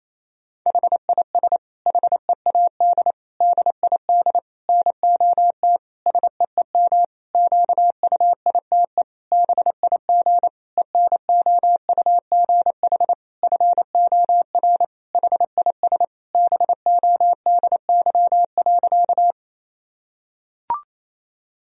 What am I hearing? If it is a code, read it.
Morse code, heard as a keyed tone, about 28 wpm, 700 Hz. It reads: HIS HEAD DID NOT SEEM QUITE BIG ENOUGH FOR HIS BODY.
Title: Sentences from Top 1000 Words - Review Part 2 - 28wpm